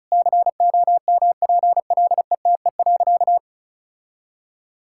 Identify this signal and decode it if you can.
Morse code, heard as a keyed tone, about 34 wpm, 700 Hz. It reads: COMPLETE.
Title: Sentences from Top 1000 Words - Review Part 6 - 35wpm